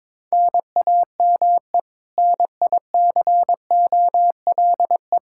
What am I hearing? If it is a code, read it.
Morse code, heard as a keyed tone, about 21 wpm, 700 Hz. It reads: NAME NICOLE
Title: Practice QSO - Exchange 1 - Beginning - 22wpm